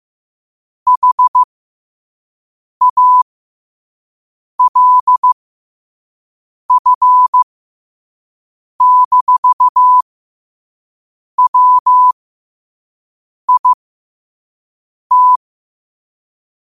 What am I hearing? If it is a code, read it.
Morse code, heard as a keyed tone, about 15 wpm, spaced as for 5 wpm, 1000 Hz. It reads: HALF-WIT
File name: qotdfiles / 5wpm